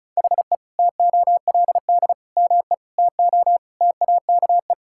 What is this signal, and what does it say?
Morse code, heard as a keyed tone, about 36 wpm, 700 Hz. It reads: HE TOLD ME TO TAKE
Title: Sentences from Top 1000 Words - Review Part 6 - 35wpm